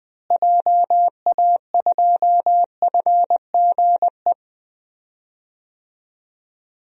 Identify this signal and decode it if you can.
Morse code, heard as a keyed tone, about 20 wpm, 700 Hz. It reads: JA2FGE
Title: DXCC-100 Call Signs - 6 Characters - Repeated 2x - 20wpm